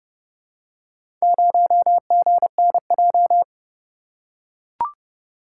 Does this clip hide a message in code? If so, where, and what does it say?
Yes: Morse code from 1.2 to 3.4 s, heard as a keyed tone, about 29 wpm, 700 Hz. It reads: 0GNJ